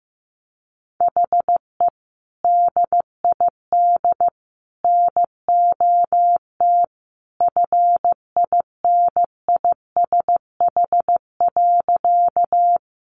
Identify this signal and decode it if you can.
Morse code, heard as a keyed tone, about 15 wpm, 700 Hz. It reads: HE DID NOT FINISH.
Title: Sentences from Top 1200 Words 15wpm